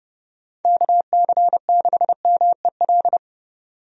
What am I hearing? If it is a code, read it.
Morse code, heard as a keyed tone, about 32 wpm, 700 Hz. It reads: KC6MEL
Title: USA Call Signs - 6 Characters - 30wpm